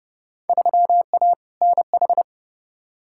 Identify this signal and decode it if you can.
Morse code, heard as a keyed tone, about 30 wpm, 700 Hz. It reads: 3A NH
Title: Field Day Exchanges - 30wpm